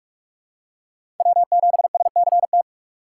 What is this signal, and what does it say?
Morse code, heard as a keyed tone, about 45 wpm, 700 Hz. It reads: W7SCT
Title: USA Call Signs - 3x Character Spacing - 45wpm